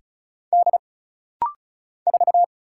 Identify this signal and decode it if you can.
Morse code, heard as a keyed tone, about 36 wpm, 700 Hz. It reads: D 4
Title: Single Letter-Number - Rapid-Fire 35wpm